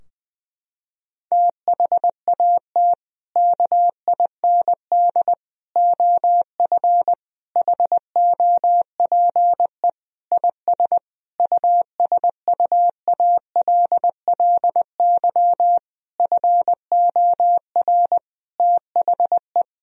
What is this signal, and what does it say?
Morse code, heard as a keyed tone, about 21 wpm, 700 Hz. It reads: THAT KIND OF HOPE IS USUALLY FOR THE
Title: Sentences from Top 1000 Words - Review Part 3 - 20wpm